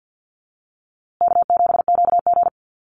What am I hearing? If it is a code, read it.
Morse code, heard as a keyed tone, about 50 wpm, 700 Hz. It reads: K7QG
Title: USA Call Signs Repeated 3x 50wpm